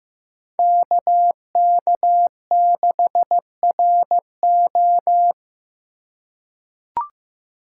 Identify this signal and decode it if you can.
Morse code, heard as a keyed tone, about 15 wpm, 700 Hz. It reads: KK6RO
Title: USA Call Signs - 5 Characters - 4x Character Spacing - 15wpm